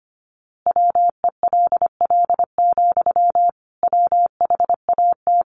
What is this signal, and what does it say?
Morse code, heard as a keyed tone, about 25 wpm, 700 Hz. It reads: WELL, WHAT
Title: A Sound Of Thunder 25wpm